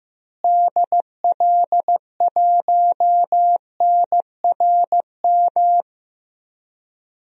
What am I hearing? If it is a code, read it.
Morse code, heard as a keyed tone, about 15 wpm, 700 Hz. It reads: DL1NRM
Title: DXCC-100 Call Signs - 6 Characters - Repeated 2x - 15wpm